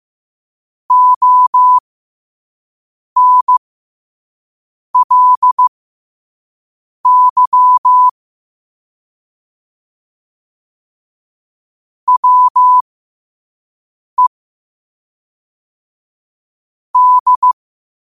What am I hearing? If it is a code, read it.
Morse code, heard as a keyed tone, about 15 wpm, spaced as for 5 wpm, 1000 Hz. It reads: ONLY WE D
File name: qotdfiles / 5wpm